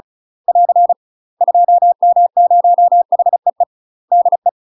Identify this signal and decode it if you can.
Morse code, heard as a keyed tone, about 36 wpm, 700 Hz. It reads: <AR> 2M0HEE DE
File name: UK MASTER QSOs TRAINING / 35WPM / UK MASTER QSOs 0033